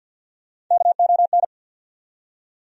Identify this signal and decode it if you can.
Morse code, heard as a keyed tone, about 53 wpm, 700 Hz. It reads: KON
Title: USA Call Sign Postfixes - 3 Letters 50wpm